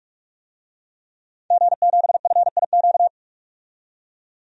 Morse code, heard as a keyed tone, about 45 wpm, 700 Hz. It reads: G7FIQ
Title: DXCC-100 Call Signs - 5 Characters - 45wpm